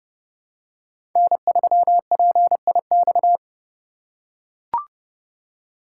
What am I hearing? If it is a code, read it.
Morse code, heard as a keyed tone, about 30 wpm, 700 Hz. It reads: N3PIX
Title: USA Call Signs - 5 Characters - 3x Character Spacing - 30wpm